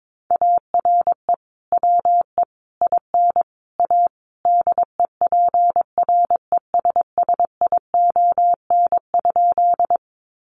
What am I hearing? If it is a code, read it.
Morse code, heard as a keyed tone, about 22 wpm, 700 Hz. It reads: ARE WE IN A DEPRESSION?
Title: Sentences from Top 1300 Words 22wpm